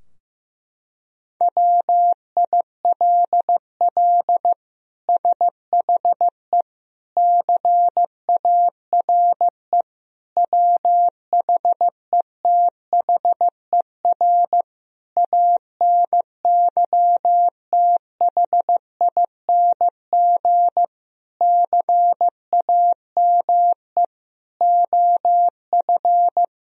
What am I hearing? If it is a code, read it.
Morse code, heard as a keyed tone, about 16 wpm, 700 Hz. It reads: WILL SHE CARE WHETHER ANYTHING CAME OF
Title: Sentences from Top 1000 Words - Review Part 3 - 15wpm